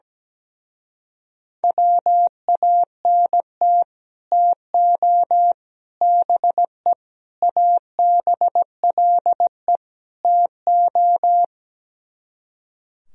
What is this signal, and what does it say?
Morse code, heard as a keyed tone, about 17 wpm, 700 Hz. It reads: WANT TO BE ABLE TO
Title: Speed-Racing - Sets of 5 Words 17wpm